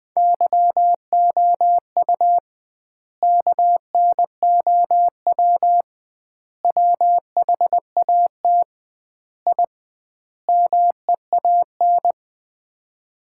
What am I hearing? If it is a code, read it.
Morse code, heard as a keyed tone, about 20 wpm, 700 Hz. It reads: YOU KNOW WHAT I MEAN
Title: 2x Word-Spacing - Sets of 5 Words - 20wpm